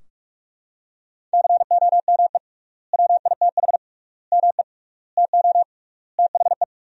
Morse code, heard as a keyed tone, about 42 wpm, 700 Hz. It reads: COME WITH ME TO THE
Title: 3x Word-Spacing - Sentences from Top 200 Words - 45wpm